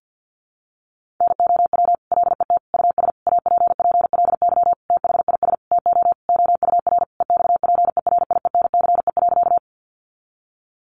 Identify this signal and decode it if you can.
Morse code, heard as a keyed tone, about 50 wpm, 700 Hz. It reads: NOW LET US APPLY THIS TO OUR EXPERIENCE.
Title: Sentences from Top 1200 Words 50wpm